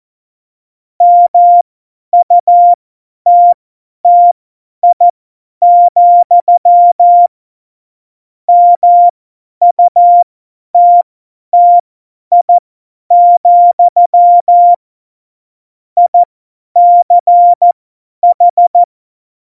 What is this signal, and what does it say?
Morse code, heard as a keyed tone, about 14 wpm, spaced as for 10 wpm, 700 Hz. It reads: MUTTI, MUTTI, ICH
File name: Morsekurs / Witze / Witz 21. November 2017, 50 F70